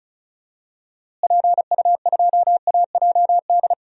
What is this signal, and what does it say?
Morse code, heard as a keyed tone, about 36 wpm, 700 Hz. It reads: PU2AJD
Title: DXCC-100 Call Signs - 6 Characters - Repeated 3x - 35wpm